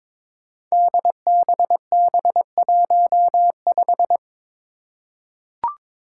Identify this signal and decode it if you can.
Morse code, heard as a keyed tone, about 22 wpm, 700 Hz. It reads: DBB15
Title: Five Challenging Letter-Number Combinations 22wpm